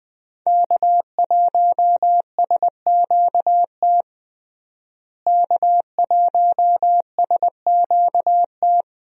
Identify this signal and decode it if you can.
Morse code, heard as a keyed tone, about 20 wpm, 700 Hz. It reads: K1SQT K1SQT
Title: USA Call Signs - 5 Characters - Repeated 3x - 20wpm